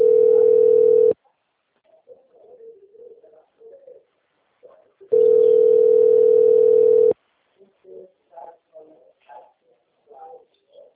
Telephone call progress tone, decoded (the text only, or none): ringback tone